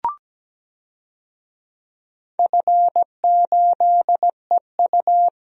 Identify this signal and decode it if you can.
Morse code, heard as a keyed tone, about 17 wpm, 700 Hz. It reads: F8EU
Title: DXCC-100 Call Signs - 4 Characters - Repeated 2x - 17wpm